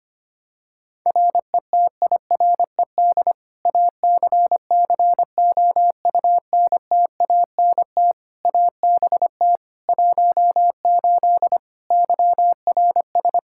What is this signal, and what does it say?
Morse code, heard as a keyed tone, about 25 wpm, 700 Hz. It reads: RETIRED ACCOUNTANT ABT 18 YRS
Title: K1USN SST QSO Practice 25wpm